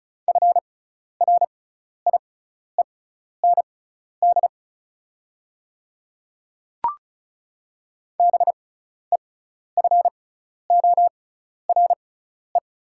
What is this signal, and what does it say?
Morse code, heard as a keyed tone, about 36 wpm, spaced as for 12 wpm, 700 Hz. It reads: FRIEND BEFORE
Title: 6x Character Spacing - Top 300 Words - 35wpm